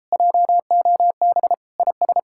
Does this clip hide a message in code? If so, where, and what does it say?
Yes: Morse code from 0.1 to 2.2 s, heard as a keyed tone, about 33 wpm, 700 Hz. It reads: JOB IS